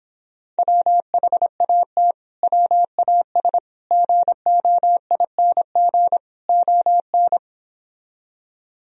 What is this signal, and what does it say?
Morse code, heard as a keyed tone, about 26 wpm, 700 Hz. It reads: WHAT WAS GOING ON